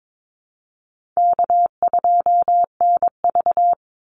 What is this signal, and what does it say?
Morse code, heard as a keyed tone, about 22 wpm, 700 Hz. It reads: K2NV
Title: Easy Call Signs Repeated 2x - 22wpm